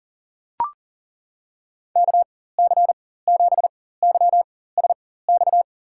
Morse code, heard as a keyed tone, about 40 wpm, spaced as for 19 wpm, 700 Hz. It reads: KC7YSX
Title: USA Call Signs - 4x Character Spacing - 40wpm